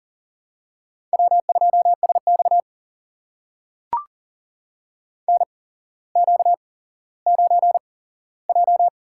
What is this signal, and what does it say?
Morse code, heard as a keyed tone, about 40 wpm, spaced as for 17 wpm, 700 Hz. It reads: W2SX NQ9J
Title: USA Call Signs - 8x Character Spacing - 40wpm